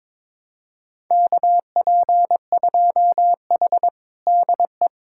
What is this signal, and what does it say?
Morse code, heard as a keyed tone, about 22 wpm, 700 Hz. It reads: KP2H DE
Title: USA Callsign DE Callsign - 4 Characters - 22wpm